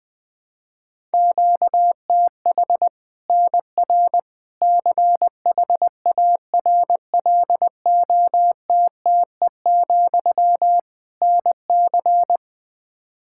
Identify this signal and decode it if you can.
Morse code, heard as a keyed tone, about 21 wpm, 700 Hz. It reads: QTH NR CHARLOTTE, NC
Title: Practice QSO - Exchange 1 - Beginning - 20wpm